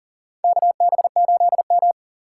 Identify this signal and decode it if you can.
Morse code, heard as a keyed tone, about 40 wpm, 700 Hz. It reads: KB8M
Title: K1USN SST QSO Practice - 40wpm